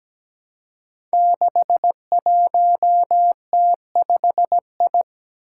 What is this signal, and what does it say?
Morse code, heard as a keyed tone, about 17 wpm, 700 Hz. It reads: 61T5I
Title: Five Challenging Letter-Number Combinations 17wpm